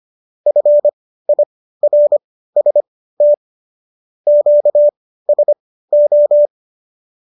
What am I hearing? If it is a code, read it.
Morse code, heard as a keyed tone, about 25 wpm, spaced as for 15 wpm, 590 Hz. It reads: FIRST QSO